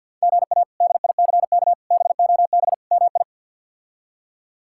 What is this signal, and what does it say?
Morse code, heard as a keyed tone, about 48 wpm, 700 Hz. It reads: GA DICK BOB MI
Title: K1USN SST QSO Practice - 50wpm